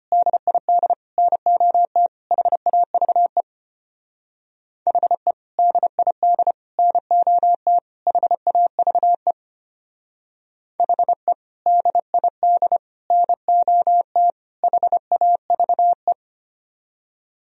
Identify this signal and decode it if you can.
Morse code, heard as a keyed tone, about 32 wpm, 700 Hz. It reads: DID NOT HAVE HE DID NOT HAVE HE DID NOT HAVE